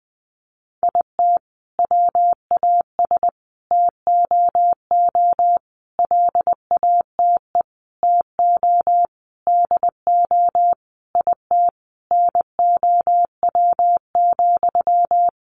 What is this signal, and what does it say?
Morse code, heard as a keyed tone, about 20 wpm, 700 Hz. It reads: IT WAS TOO LATE TO DO IT NOW,
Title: Sentences from Top 600 Words 20wpm